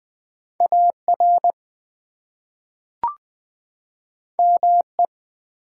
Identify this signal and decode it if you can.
Morse code, heard as a keyed tone, about 20 wpm, 700 Hz. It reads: AR ME
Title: US States and Canadian Provinces 20wpm